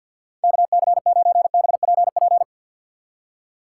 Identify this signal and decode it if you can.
Morse code, heard as a keyed tone, about 52 wpm, 700 Hz. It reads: KC9BPP
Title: USA Call Signs - 6 Characters - 3x Character Spacing - 50wpm